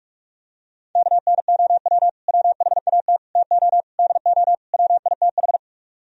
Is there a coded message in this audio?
Yes: Morse code from 1.0 to 5.6 s, heard as a keyed tone, about 46 wpm, 700 Hz. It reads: KNOW WHAT TO DO WITH